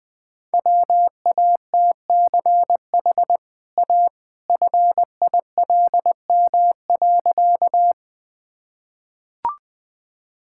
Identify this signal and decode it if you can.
Morse code, heard as a keyed tone, about 21 wpm, 700 Hz. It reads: WATCH A FILM.
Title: Easy Sentences with Hints - 20wpm